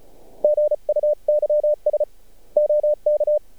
Morse code, heard as a keyed tone, about 48 wpm, spaced as for 37 wpm, 600 Hz. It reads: GUYS OK
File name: archived / CRSnet